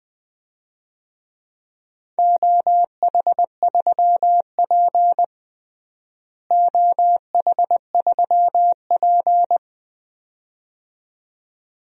Morse code, heard as a keyed tone, about 20 wpm, 700 Hz. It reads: OH3P OH3P
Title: DXCC-100 Call Signs - 4 Characters - Repeated 2x - 20wpm